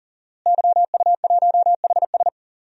Morse code, heard as a keyed tone, about 38 wpm, 700 Hz. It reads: YU1HS
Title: DXCC-100 Call Signs - 5 Characters - Repeated 2x - 40wpm